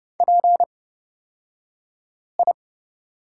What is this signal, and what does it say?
Morse code, heard as a keyed tone, about 30 wpm, spaced as for 5 wpm, 700 Hz. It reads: PI
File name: Lesson 36d - BT - Two Characters 30wpm